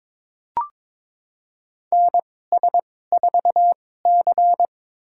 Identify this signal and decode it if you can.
Morse code, heard as a keyed tone, about 22 wpm, spaced as for 16 wpm, 700 Hz. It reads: NS4C
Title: USA Call Signs - 4 Characters - 2x Character Spacing - 22wpm